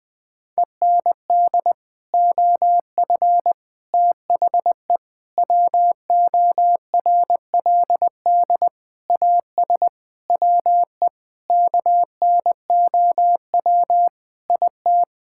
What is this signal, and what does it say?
Morse code, heard as a keyed tone, about 20 wpm, 700 Hz. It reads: END OF THE WORLD AS WE KNOW IT
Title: Sets of 8 Words - 20wpm